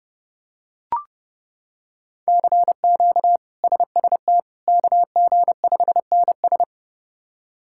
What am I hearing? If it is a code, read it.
Morse code, heard as a keyed tone, about 30 wpm, 700 Hz. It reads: CQ SST KG5NS
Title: K1USN SST QSO Practice - 30wpm